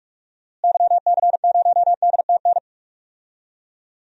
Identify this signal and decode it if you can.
Morse code, heard as a keyed tone, about 43 wpm, 700 Hz. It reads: YC0DTN
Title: DXCC-100 Call Signs - 6 Characters - Repeated 2x - 45wpm